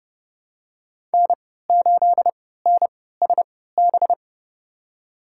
Morse code, heard as a keyed tone, about 30 wpm, spaced as for 17 wpm, 700 Hz. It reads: N8NSB
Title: USA Call Signs - 5 Characters - 3x Character Spacing - 30wpm